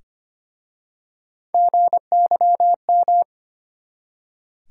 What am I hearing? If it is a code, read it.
Morse code, heard as a keyed tone, about 25 wpm, 700 Hz. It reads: GYM
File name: practice / 25wpm / Easy Words with Hints - 25wpm